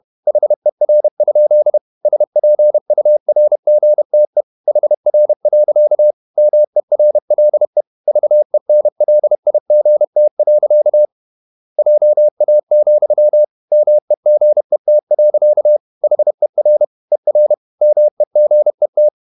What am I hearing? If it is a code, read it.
Morse code, heard as a keyed tone, about 31 wpm, 600 Hz. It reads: HER? SPURGTE HR. MERLE VENLIGT. JA, MEGET. HER ER MEGET
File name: Ferie QRQ 31wpm